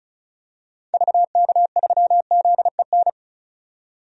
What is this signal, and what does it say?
Morse code, heard as a keyed tone, about 34 wpm, 700 Hz. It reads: VK3ZEN
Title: DXCC-100 Call Signs - 6 Characters - 35wpm